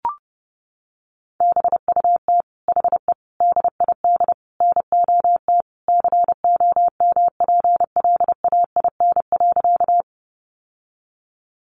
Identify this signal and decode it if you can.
Morse code, heard as a keyed tone, about 30 wpm, 700 Hz. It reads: BUT HE DID NOT COMPLAIN.
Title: Sentences from Top 1300 Words 30wpm